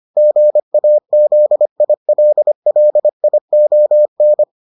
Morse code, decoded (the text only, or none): GAZILLION